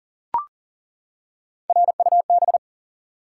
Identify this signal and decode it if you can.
Morse code, heard as a keyed tone, about 40 wpm, 700 Hz. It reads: RUB
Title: Three Letter Words - 40wpm